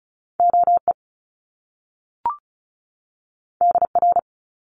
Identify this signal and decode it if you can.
Morse code, heard as a keyed tone, about 36 wpm, 700 Hz. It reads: OE DR
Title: Top Two Letter Combinations in Words 35wpm